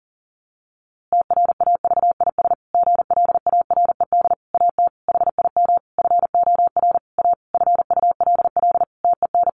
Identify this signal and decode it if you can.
Morse code, heard as a keyed tone, about 40 wpm, 700 Hz. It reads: TRAVIS GLARED AT HIM FOR A FULL TEN
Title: A Sound Of Thunder 40wpm